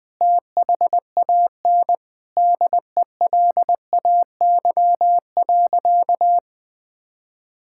Morse code, heard as a keyed tone, about 20 wpm, 700 Hz. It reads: THAN DELAY.
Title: Sentences with 2-Word Phrases - Spoken Beginning - 20wpm